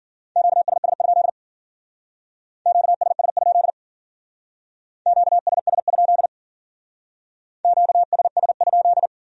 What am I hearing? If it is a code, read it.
Morse code, heard as a keyed tone, about 48 wpm, 700 Hz. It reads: QSS? QSS? QSS? QSS?
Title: Speed-Racing Q-Codes - 40wpm